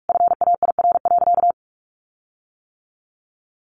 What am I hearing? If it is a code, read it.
Morse code, heard as a keyed tone, about 44 wpm, 700 Hz. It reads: FAIR.
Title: Sentences from Top 1300 Words 45wpm